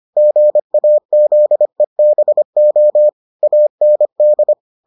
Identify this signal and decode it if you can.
Morse code, heard as a keyed tone, about 25 wpm, 600 Hz. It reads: GAZEBO AND